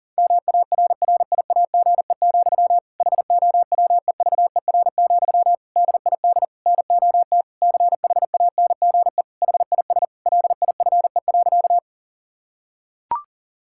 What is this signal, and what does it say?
Morse code, heard as a keyed tone, about 37 wpm, 700 Hz. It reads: MARRIAGE, HOWEVER, DID NOT CHANGE HIS LIFE.
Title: Sentences from Top 1000 Words - Review Part 5 - 40wpm